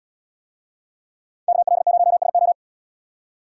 Morse code, heard as a keyed tone, about 77 wpm, 700 Hz. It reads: KL0SO